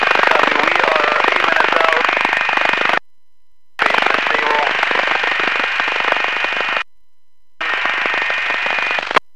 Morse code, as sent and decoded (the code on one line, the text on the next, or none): none